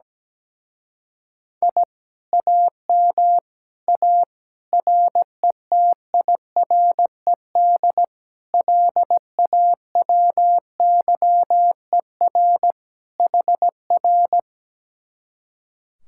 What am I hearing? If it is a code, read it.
Morse code, heard as a keyed tone, about 17 wpm, 700 Hz. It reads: I AM A RETIRED LAWYER HR
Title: Speed-Racing Common QSO Phrases - 17wpm